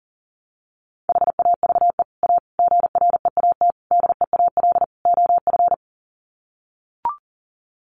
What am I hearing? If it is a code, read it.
Morse code, heard as a keyed tone, about 40 wpm, 700 Hz. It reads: HAVE A GREAT DEAL OF
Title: Sets of 5 Words Encore2 40wpm